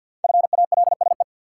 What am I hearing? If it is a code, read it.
Morse code, heard as a keyed tone, about 51 wpm, 700 Hz. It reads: FALSE
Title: Sentences from Top 1600 Words - New Word Drill Alt - 50wpm